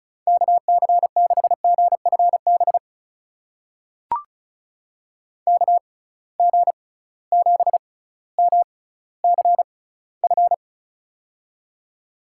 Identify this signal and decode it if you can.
Morse code, heard as a keyed tone, about 36 wpm, spaced as for 18 wpm, 700 Hz. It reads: KC6GFB KG7MCF
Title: USA Call Signs - 6 Characters - 6x Character Spacing - 35wpm